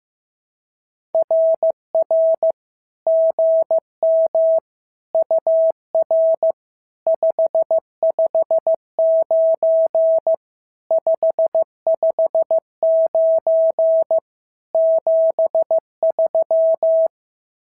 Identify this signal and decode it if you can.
Morse code, heard as a keyed tone, about 14 wpm, 650 Hz. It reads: RR GM UR 559 559 73
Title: SOTA QSO Practice 15wpm